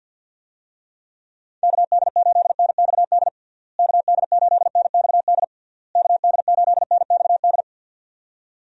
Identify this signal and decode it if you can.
Morse code, heard as a keyed tone, about 51 wpm, 700 Hz. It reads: KD8NXD KD8NXD KD8NXD
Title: USA Call Signs - 6 Characters - Repeated 3x - 50wpm